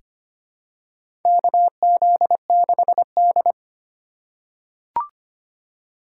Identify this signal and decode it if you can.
Morse code, heard as a keyed tone, about 25 wpm, 700 Hz. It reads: KZ6D
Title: USA Call Signs - 4 Characters - 2x Character Spacing - 25wpm